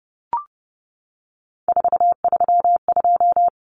Morse code, heard as a keyed tone, about 30 wpm, 700 Hz. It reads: 432
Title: Numbers - 3-Digit 30wpm